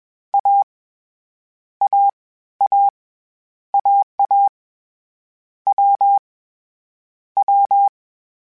Morse code, heard as a keyed tone, about 21 wpm, 800 Hz. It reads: A A A AA W W
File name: Difficult Combos / Difficult Combos-mp3s / AWJ ShortRun@20